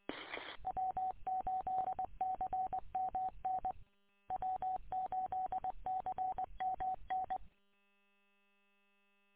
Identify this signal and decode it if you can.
Morse code, heard as a keyed tone, about 20 wpm, 750 Hz. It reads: W8CMN W8CMN
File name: w8cmn- -combined